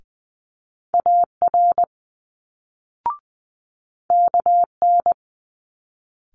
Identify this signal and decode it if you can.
Morse code, heard as a keyed tone, about 20 wpm, 700 Hz. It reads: AR KN
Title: Multi-letter Phonemes Quick Fire 20wpm